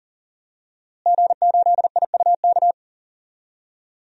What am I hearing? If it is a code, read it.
Morse code, heard as a keyed tone, about 40 wpm, 700 Hz. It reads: G8IUK
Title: DXCC-100 Call Signs - 5 Characters - Repeated 3x - 40wpm